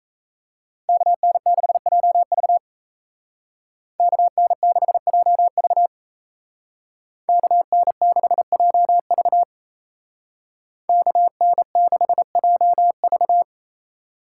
Morse code, heard as a keyed tone, about 36 wpm, 700 Hz. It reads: KN6JV KN6JV KN6JV KN6JV